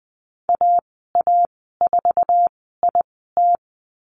Morse code, heard as a keyed tone, about 20 wpm, spaced as for 15 wpm, 700 Hz. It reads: AA4IT
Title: 048 Lesson 9g - Number 4 - Call Signs 20wpm